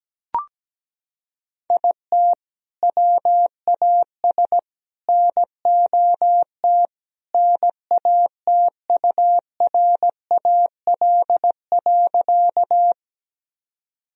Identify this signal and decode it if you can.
Morse code, heard as a keyed tone, about 17 wpm, 700 Hz. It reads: IT WAS NOT NATURAL.
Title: Sentences from Top 1000 Words - Review Part 4 - 17wpm